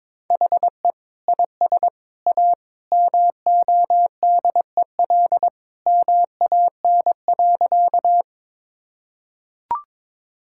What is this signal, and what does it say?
Morse code, heard as a keyed tone, about 22 wpm, 700 Hz. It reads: HE IS A MODEL MAN.
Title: Sentences from Top 1000 Words - Review Part 5 - 22wpm